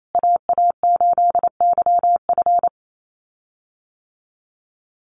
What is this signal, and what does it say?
Morse code, heard as a keyed tone, about 28 wpm, 700 Hz. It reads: AA8YF
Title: USA Call Signs Repeated 3x 28wpm